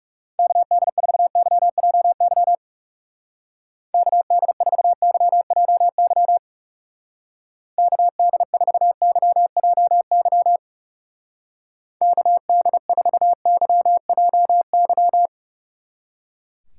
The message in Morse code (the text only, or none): KD4YJY KD4YJY KD4YJY KD4YJY